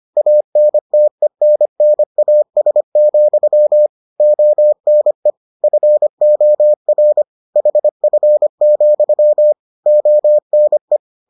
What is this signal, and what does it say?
Morse code, heard as a keyed tone, about 25 wpm, 600 Hz. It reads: ANTENNAS, ONE FOR HF, ONE